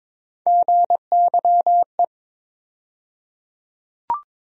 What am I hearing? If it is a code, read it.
Morse code, heard as a keyed tone, about 21 wpm, 700 Hz. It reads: GYE
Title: USA Call Sign Postfixes - 3 Letters 22wpm